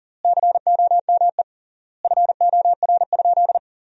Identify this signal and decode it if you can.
Morse code, heard as a keyed tone, about 40 wpm, 700 Hz. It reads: COME FOR?
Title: 3x Word-Spacing - Sentences from Top 100 Words - 40wpm